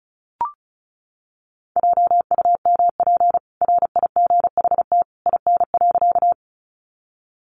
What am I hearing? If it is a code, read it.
Morse code, heard as a keyed tone, about 34 wpm, 700 Hz. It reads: JUMP RIGHT IN.
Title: Sentences from Top 1300 Words 35wpm